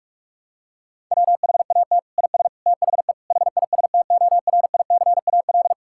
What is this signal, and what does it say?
Morse code, heard as a keyed tone, about 41 wpm, 700 Hz. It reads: WHAT IS THE HISTORICAL
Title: Sentences from Top 1000 Words - Review Part 5 - 45wpm